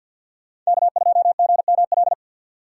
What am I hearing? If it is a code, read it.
Morse code, heard as a keyed tone, about 61 wpm, 700 Hz. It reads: K2GML
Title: Speed Racing - USA 1x3 Call Signs 50wpm